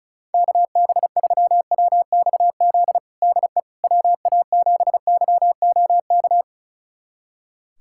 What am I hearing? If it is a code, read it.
Morse code, heard as a keyed tone, about 34 wpm, 700 Hz. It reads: KB3WXZ DE WA7YOK
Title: USA Callsign DE Callsign - 6 Characters - 35wpm